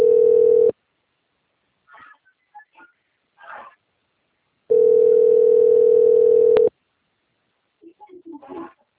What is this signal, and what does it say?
Ringback tone (North America), 440 + 480 Hz.